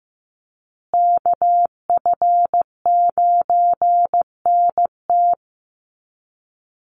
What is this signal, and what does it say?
Morse code, heard as a keyed tone, about 15 wpm, 700 Hz. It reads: KF9NT
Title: USA Call Signs Repeated 2x 15wpm